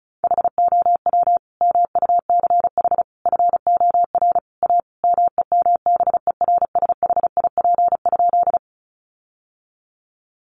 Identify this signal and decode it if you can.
Morse code, heard as a keyed tone, about 34 wpm, 700 Hz. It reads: HOW MUCH FOR A MEMBERSHIP?
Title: Sentences from Top 1400 Words 35wpm